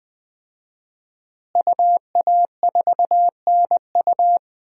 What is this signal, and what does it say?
Morse code, heard as a keyed tone, about 20 wpm, 700 Hz. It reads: UA4NU
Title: DXCC-100 Call Signs - 5 Characters - Repeated 3x - 20wpm